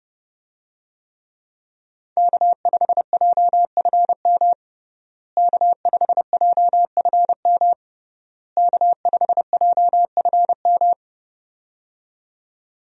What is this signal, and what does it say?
Morse code, heard as a keyed tone, about 28 wpm, 700 Hz. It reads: K5JFM K5JFM K5JFM
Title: USA Call Signs - 5 Characters - Repeated 3x - 30wpm